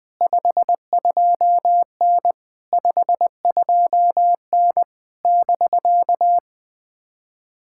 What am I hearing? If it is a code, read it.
Morse code, heard as a keyed tone, about 20 wpm, 700 Hz. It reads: 52N 52N <BK>
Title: POTA QSO Practice - 20wpm